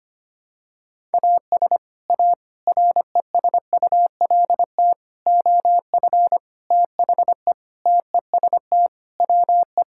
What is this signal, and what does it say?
Morse code, heard as a keyed tone, about 25 wpm, 700 Hz. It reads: AS A RESULT OF THE TEST WE
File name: practice / 25wpm / Sentences with 4-Word Phrases - 25wpm